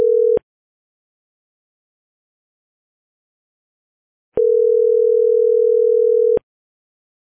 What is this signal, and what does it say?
Ringback tone (North America), 440 + 480 Hz.